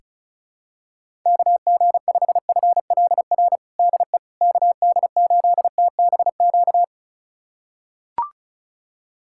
Morse code, heard as a keyed tone, about 36 wpm, 700 Hz. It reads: KG5FLR DE KD8TBQ
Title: USA Callsign DE Callsign 35wpm